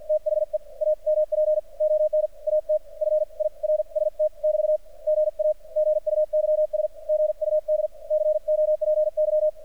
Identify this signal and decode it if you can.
Morse code, heard as a keyed tone, about 46 wpm, 610 Hz. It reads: THE AMP ON AT FIRST <BT> GA GUYS GUD COPY
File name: archived / CRSnet